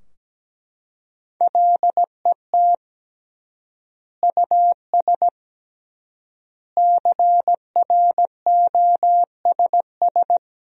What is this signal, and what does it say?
Morse code, heard as a keyed tone, about 17 wpm, 700 Hz. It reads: LET US CROSS
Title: 3x Word-Spacing - Sentences from Top 300 Words - 17wpm